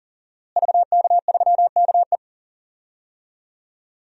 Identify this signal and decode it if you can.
Morse code, heard as a keyed tone, about 37 wpm, 700 Hz. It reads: VK3KE
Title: DXCC-100 Call Signs - 5 Characters - Repeated 3x - 40wpm